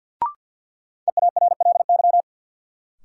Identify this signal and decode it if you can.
Morse code, heard as a keyed tone, about 47 wpm, 700 Hz. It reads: EARLY
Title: Sentences from Top 400 Words - New Word Drill - 50wpm